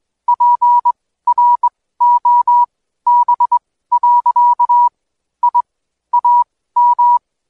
Morse code, heard as a keyed tone, about 21 wpm, spaced as for 15 wpm, 970 Hz. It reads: PROB. I AM
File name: Code Quick / AudioDownld2 / Seg125-DayDrill30-16WPM